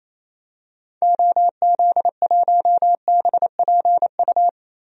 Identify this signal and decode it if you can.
Morse code, heard as a keyed tone, about 28 wpm, 700 Hz. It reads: OZ1BPU
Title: DXCC-100 Call Signs - 6 Characters - 28wpm